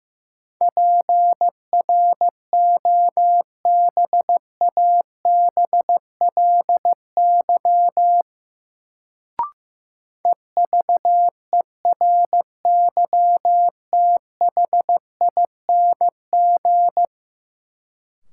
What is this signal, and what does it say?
Morse code, heard as a keyed tone, about 15 wpm, 700 Hz. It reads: PROBABLY EVERYTHING
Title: Sentences from Top 700 Words - New Word Drill - 15wpm